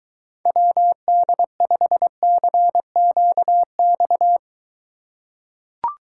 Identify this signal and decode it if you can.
Morse code, heard as a keyed tone, about 22 wpm, 700 Hz. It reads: WD5CQX